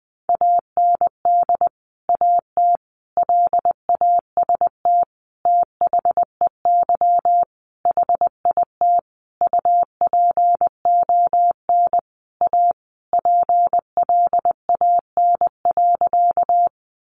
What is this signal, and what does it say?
Morse code, heard as a keyed tone, about 20 wpm, 700 Hz. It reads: AND AT LAST THEY HIT UPON A PLAN.
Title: Sentences from Top 1200 Words 20wpm